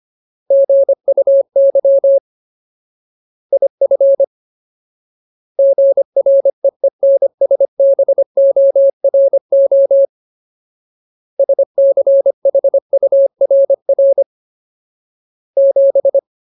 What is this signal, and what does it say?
Morse code, heard as a keyed tone, about 25 wpm, 550 Hz. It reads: GUY IF GREENSBORO SCHURR 7
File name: Ultimate Word List / 25WPM Ultimate